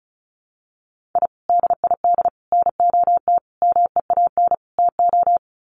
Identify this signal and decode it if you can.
Morse code, heard as a keyed tone, about 34 wpm, 700 Hz. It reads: I DID NOT MEAN TO
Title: Sets of 5 Words 35wpm